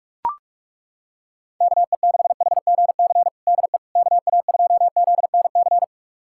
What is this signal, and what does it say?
Morse code, heard as a keyed tone, about 42 wpm, 700 Hz. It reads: KE6HGC DE KA2ZNC
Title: USA Callsign DE Callsign - 6 Characters - 45wpm